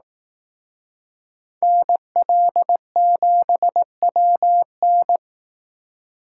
Morse code, heard as a keyed tone, about 18 wpm, 700 Hz. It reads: NL7WN
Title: Speed Racing - USA 2x2 Call Signs 15wpm